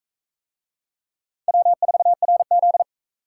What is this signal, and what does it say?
Morse code, heard as a keyed tone, about 40 wpm, 700 Hz. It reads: W4RZ